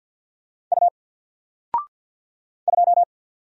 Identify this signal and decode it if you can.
Morse code, heard as a keyed tone, about 54 wpm, 700 Hz. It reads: U 2
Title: Single Character (Letter:Number:Punctuation:Prosign) 50wpm